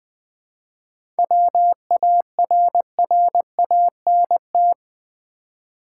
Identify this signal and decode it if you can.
Morse code, heard as a keyed tone, about 20 wpm, 700 Hz. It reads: WARRANT
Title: 7 Letter Words - First 3 Letters Given - 20wpm